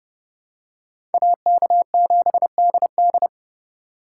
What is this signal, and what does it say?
Morse code, heard as a keyed tone, about 29 wpm, 700 Hz. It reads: AK7DD